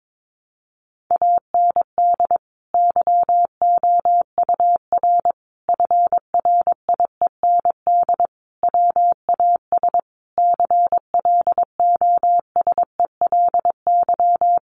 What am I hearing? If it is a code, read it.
Morse code, heard as a keyed tone, about 22 wpm, 700 Hz. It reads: AND YOUR FRIEND WAS CLOSELY
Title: Sentences from Top 1400 Words 22wpm